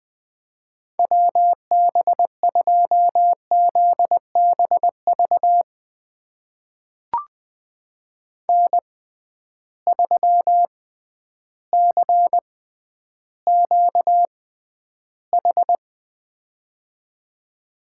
Morse code, heard as a keyed tone, about 20 wpm, spaced as for 11 wpm, 700 Hz. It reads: WB2ZBV N3CQH
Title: USA Call Signs - 6x Character Spacing - 20wpm